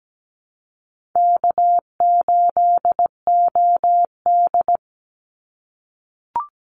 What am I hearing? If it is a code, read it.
Morse code, heard as a keyed tone, about 17 wpm, 700 Hz. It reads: K8OD
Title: Easy Call Signs - 17wpm